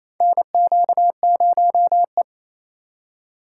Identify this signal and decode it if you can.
Morse code, heard as a keyed tone, about 28 wpm, 700 Hz. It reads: NQ0E
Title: USA Call Signs - 4 Characters - 2x Character Spacing - 28wpm